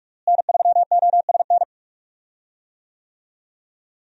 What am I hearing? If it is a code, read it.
Morse code, heard as a keyed tone, about 45 wpm, 700 Hz. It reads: N3OSN
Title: USA Call Signs - 5 Characters - Repeated 2x - 45wpm